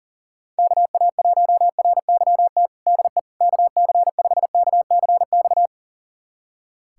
Morse code, heard as a keyed tone, about 40 wpm, 700 Hz. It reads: KA1RYT DE KC5KCX
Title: USA Callsign DE Callsign - 6 Characters - 40wpm